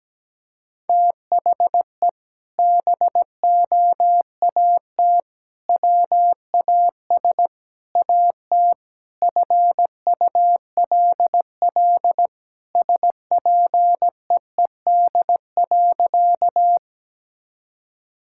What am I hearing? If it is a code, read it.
Morse code, heard as a keyed tone, about 17 wpm, 700 Hz. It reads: THE BOAT WAS AT FULL SPEED.
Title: Sentences from Top 1000 Words - Review Part 5 - 17wpm